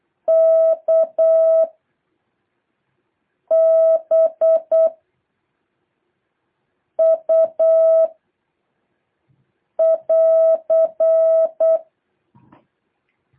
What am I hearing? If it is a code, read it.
Morse code, heard as a keyed tone, about 8 wpm, 650 Hz. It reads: K B U <AR>
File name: jh8cbh / 2023cw